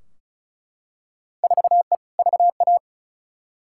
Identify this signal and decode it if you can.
Morse code, heard as a keyed tone, about 34 wpm, 700 Hz. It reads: 4E VA
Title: Field Day Exchanges Encore - 35wpm